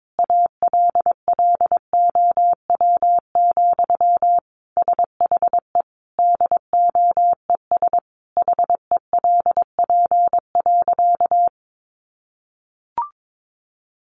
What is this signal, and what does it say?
Morse code, heard as a keyed tone, about 22 wpm, 700 Hz. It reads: ALLOW, SHE DOES HELP.
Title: Sentences from Top 700 Words 22wpm